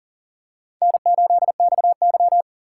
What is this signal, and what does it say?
Morse code, heard as a keyed tone, about 42 wpm, 700 Hz. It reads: N8XY
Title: USA Call Signs - 4 Characters - 3x Character Spacing - 40wpm